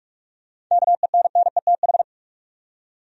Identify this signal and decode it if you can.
Morse code, heard as a keyed tone, about 45 wpm, 700 Hz. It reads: KENNETH
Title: USA Common Names - 45wpm